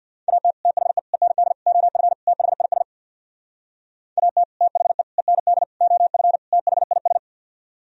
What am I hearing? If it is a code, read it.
Morse code, heard as a keyed tone, about 55 wpm, 700 Hz. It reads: AT THE END OF THIS AT THE END OF THIS
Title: Speed-Racing - Sets of 5 Words 50wpm